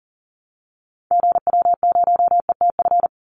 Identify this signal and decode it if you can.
Morse code, heard as a keyed tone, about 40 wpm, 700 Hz. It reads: GW0ETF
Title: Call Signs Repeated 2x - 40wpm